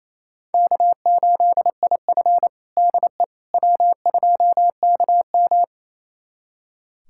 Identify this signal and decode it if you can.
Morse code, heard as a keyed tone, about 28 wpm, 700 Hz. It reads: K8IF DE W2KM
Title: USA Callsign DE Callsign - 4 Characters - 28wpm